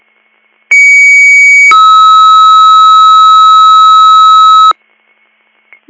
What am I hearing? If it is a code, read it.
Two-tone page, heard as a one-second tone, then a three-second tone, about 2250 then 1290 Hz. It is heard from 0.7 to 4.7 s.